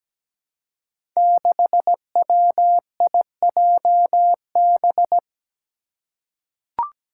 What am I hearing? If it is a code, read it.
Morse code, heard as a keyed tone, about 17 wpm, 700 Hz. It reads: 6WIJB
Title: Five Challenging Letter-Number Combinations 17wpm